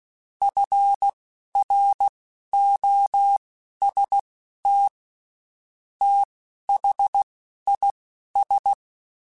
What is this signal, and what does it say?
Morse code, heard as a keyed tone, about 16 wpm, spaced as for 11 wpm, 800 Hz. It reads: FROST THIS
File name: Spaced Characters / Spaced Characters-mp3s / Weather4 24 sp @15